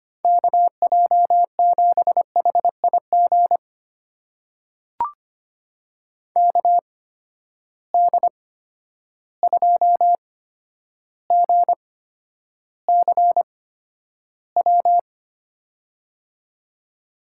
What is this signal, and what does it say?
Morse code, heard as a keyed tone, about 25 wpm, spaced as for 11 wpm, 700 Hz. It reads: KJ7HIG KD2GCW
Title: USA Call Signs - 6 Characters - 8x Character Spacing - 25wpm